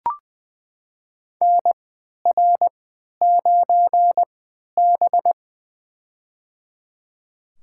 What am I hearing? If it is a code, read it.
Morse code, heard as a keyed tone, about 20 wpm, spaced as for 11 wpm, 700 Hz. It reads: NR9B
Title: USA Call Signs - 4 Characters - 3x Character Spacing - 20wpm